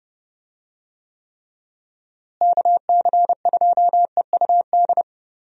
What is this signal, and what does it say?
Morse code, heard as a keyed tone, about 30 wpm, 700 Hz. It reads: KC2EUD
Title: USA Call Signs - 6 Characters - Repeated 2x - 30wpm